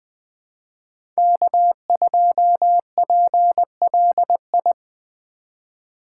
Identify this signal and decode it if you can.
Morse code, heard as a keyed tone, about 21 wpm, 700 Hz. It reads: K2PLI
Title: USA Call Signs - 5 Characters - 3x Character Spacing - 20wpm